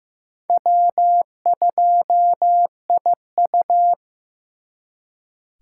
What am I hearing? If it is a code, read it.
Morse code, heard as a keyed tone, about 18 wpm, 700 Hz. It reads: W2IU
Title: Speed Racing - USA Call Signs 15wpm